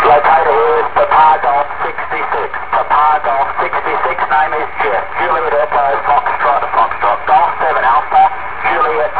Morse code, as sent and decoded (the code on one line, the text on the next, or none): none